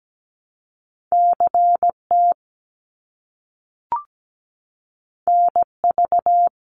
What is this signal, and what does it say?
Morse code, heard as a keyed tone, about 17 wpm, 700 Hz. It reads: CT NV
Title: US States Repeated 3x 17wpm